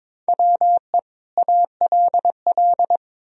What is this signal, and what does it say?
Morse code, heard as a keyed tone, about 22 wpm, 700 Hz. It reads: WE ALL
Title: Sentences from Top 1000 Words - Review Part 3 - 22wpm